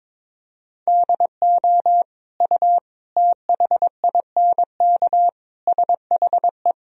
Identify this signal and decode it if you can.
Morse code, heard as a keyed tone, about 21 wpm, 700 Hz. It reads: DO U THINK SHE
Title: Sentences from Top 200 Words and Abbreviations 22wpm